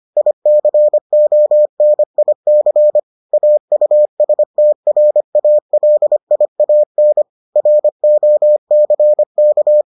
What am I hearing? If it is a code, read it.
Morse code, heard as a keyed tone, about 25 wpm, 600 Hz. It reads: ICONIC AUSTRALIAN ROCK